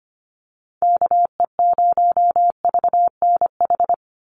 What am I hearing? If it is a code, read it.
Morse code, heard as a keyed tone, about 25 wpm, 700 Hz. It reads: KE0VNH
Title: USA Call Signs 25wpm